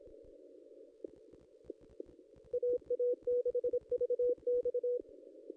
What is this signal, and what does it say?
Morse code, heard as a keyed tone, about 26 wpm, 490 Hz. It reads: AA6VX
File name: contests / ss-cw-k5tr-op-n5rz -1800